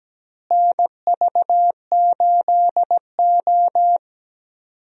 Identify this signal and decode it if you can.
Morse code, heard as a keyed tone, about 17 wpm, 700 Hz. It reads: NV8O
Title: USA Call Signs - 4 Characters - Repeated 3x - 17wpm